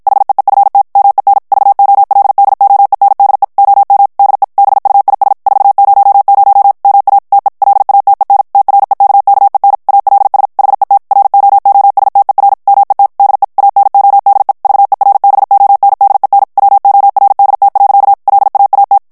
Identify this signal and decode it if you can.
Morse code, heard as a keyed tone, about 52 wpm, 800 Hz. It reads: HEEFT MEN VOLDOENDE OM DE BUIS 400 MA TE LATEN TREKKEN ALS HET ROOSTER MET DE ANODE VERBONDEN WORDT. LAAT
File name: HOLLANDAIS texte clair 2 vitesse 50 mots minute